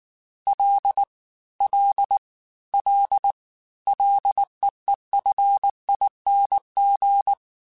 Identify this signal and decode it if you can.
Morse code, heard as a keyed tone, about 19 wpm, 800 Hz. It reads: L L L LEEFING
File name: Ls / Single L m3s / Single L Prefixe 3x2 Twice @18